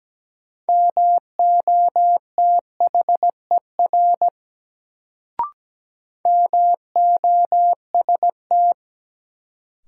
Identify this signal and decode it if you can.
Morse code, heard as a keyed tone, about 17 wpm, 700 Hz. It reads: MOTHER MOST
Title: Sentences from Top 200 Words - New Word Drill - 17wpm